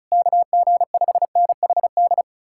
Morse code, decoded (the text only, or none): KG5NHD